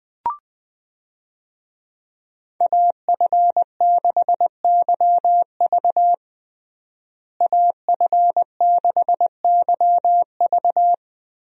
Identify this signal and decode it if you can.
Morse code, heard as a keyed tone, about 20 wpm, 700 Hz. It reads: AF6YV AF6YV
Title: USA Call Signs - 5 Characters - Repeated 2x - 20wpm